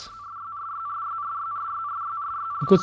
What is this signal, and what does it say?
RTTY, about 45.45 baud, with speech. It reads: RYRYRYRYRYRYRYR